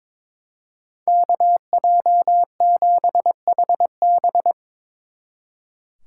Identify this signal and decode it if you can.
Morse code, heard as a keyed tone, about 24 wpm, 700 Hz. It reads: KJ7HB
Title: Speed Racing - USA 2x2 Call Signs 22wpm